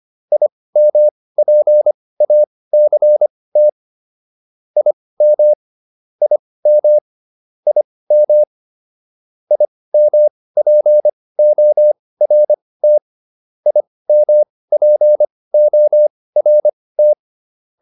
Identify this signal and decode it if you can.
Morse code, heard as a keyed tone, about 25 wpm, spaced as for 18 wpm, 600 Hz. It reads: IMPACT IM IM IM IMPORT IMPORT